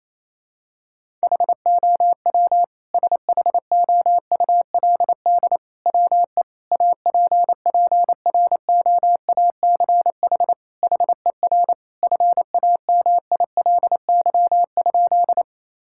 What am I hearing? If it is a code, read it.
Morse code, heard as a keyed tone, about 28 wpm, 700 Hz. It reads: HOW SHOULD WE APPROACH HER FAMILY?
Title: Sentences from Top 1000 Words - Review Part 6 - 28wpm